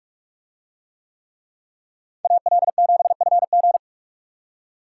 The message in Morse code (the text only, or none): AL7FG